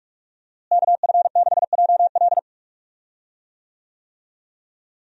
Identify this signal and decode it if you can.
Morse code, heard as a keyed tone, about 46 wpm, 700 Hz. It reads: KF6JL
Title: USA Call Signs - 5 Characters - Repeated 3x - 45wpm